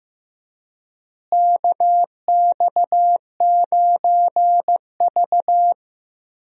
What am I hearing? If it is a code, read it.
Morse code, heard as a keyed tone, about 15 wpm, 700 Hz. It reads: KX9V
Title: USA Call Signs - 4 Characters - Repeated 2x - 15wpm